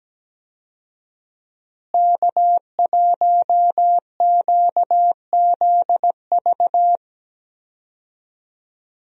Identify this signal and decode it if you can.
Morse code, heard as a keyed tone, about 17 wpm, 700 Hz. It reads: K1QZV
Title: USA Call Signs - 5 Characters - Repeated 2x - 17wpm